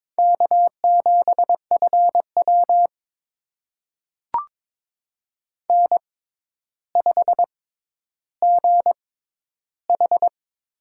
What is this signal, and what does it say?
Morse code, heard as a keyed tone, about 22 wpm, spaced as for 11 wpm, 700 Hz. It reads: K7FW N5GH
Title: USA Call Signs - 4 Characters - 6x Character Spacing - 22wpm